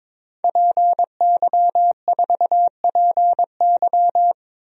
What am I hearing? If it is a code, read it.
Morse code, heard as a keyed tone, about 22 wpm, 700 Hz. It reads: PY4PY
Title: DXCC-100 Call Signs - 5 Characters - 22wpm